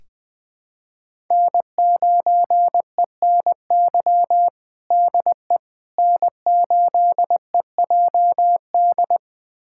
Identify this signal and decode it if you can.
Morse code, heard as a keyed tone, about 20 wpm, 700 Hz. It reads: N9ENY DE N8EJD
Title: USA Callsign DE Callsign - 5 Characters - 20wpm